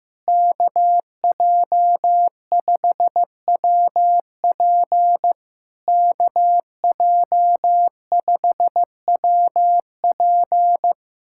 Morse code, heard as a keyed tone, about 15 wpm, 700 Hz. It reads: KJ5WP KJ5WP
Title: SOTA QSO Practice 15wpm